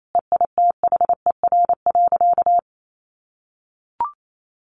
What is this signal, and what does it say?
Morse code, heard as a keyed tone, about 28 wpm, 700 Hz. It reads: EITHER.
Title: Sentences from Top 600 Words 28wpm